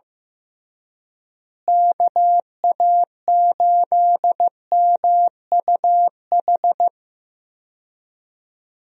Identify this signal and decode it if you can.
Morse code, heard as a keyed tone, about 15 wpm, 700 Hz. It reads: KA8MUH
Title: USA Call Signs - 6 Characters - Repeated 3x - 15wpm